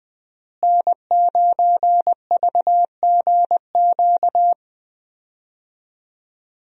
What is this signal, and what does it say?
Morse code, heard as a keyed tone, about 21 wpm, 700 Hz. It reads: N9VGQ
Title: USA Call Signs - 5 Characters - Repeated 2x - 20wpm